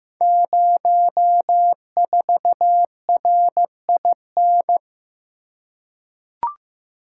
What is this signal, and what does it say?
Morse code, heard as a keyed tone, about 15 wpm, 700 Hz. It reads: 04RIN
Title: Code Groups - 5 Letter:Numbers - 15wpm